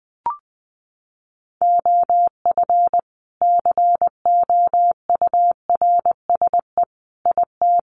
Morse code, heard as a keyed tone, about 20 wpm, 700 Hz. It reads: OF COURSE IT
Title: Sentences from Top 500 Words 20wpm